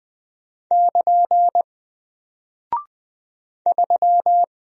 Morse code, heard as a keyed tone, about 20 wpm, 700 Hz. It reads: <KN> 3
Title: Single Character (Letter:Number:Punctuation:Prosign) 20wpm